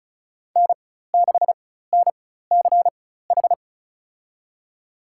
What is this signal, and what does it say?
Morse code, heard as a keyed tone, about 34 wpm, spaced as for 16 wpm, 700 Hz. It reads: N6NCH
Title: USA Call Signs - 4x Character Spacing - 35wpm